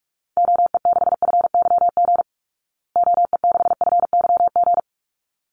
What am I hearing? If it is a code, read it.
Morse code, heard as a keyed tone, about 44 wpm, 700 Hz. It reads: OE6FYG OE6FYG
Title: Call Signs Repeated 3x - 45wpm